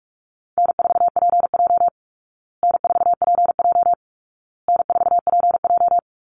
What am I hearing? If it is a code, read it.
Morse code, heard as a keyed tone, about 44 wpm, 700 Hz. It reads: N4PJ N4PJ N4PJ
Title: Easy Call Signs Repeated 3x - 45wpm